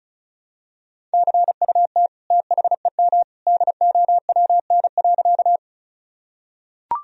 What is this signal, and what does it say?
Morse code, heard as a keyed tone, about 36 wpm, 700 Hz. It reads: CUT THEM DOWN.
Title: Sentences from Top 1000 Words - Review Part 1 - 35wpm